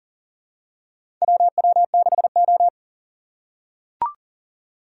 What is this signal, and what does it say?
Morse code, heard as a keyed tone, about 39 wpm, 700 Hz. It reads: WW6O
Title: USA Call Signs - 4 Characters - 2x Character Spacing - 40wpm